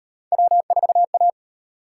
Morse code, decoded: W4A

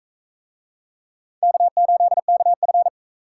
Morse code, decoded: K8KF

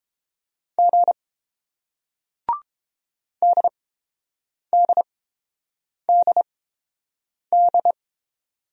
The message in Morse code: G DDDD

